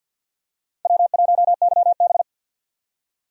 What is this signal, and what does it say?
Morse code, heard as a keyed tone, about 52 wpm, 700 Hz. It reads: W1YB
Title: USA Call Signs - 4 Characters - 2x Character Spacing - 50wpm